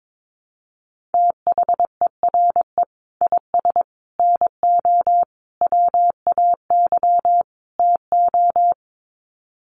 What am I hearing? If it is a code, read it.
Morse code, heard as a keyed tone, about 22 wpm, 700 Hz. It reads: THERE IS NO WAY TO
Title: Sets of 5 Words Encore 22wpm